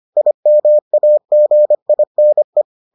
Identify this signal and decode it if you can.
Morse code, heard as a keyed tone, about 25 wpm, 600 Hz. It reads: IMAGINE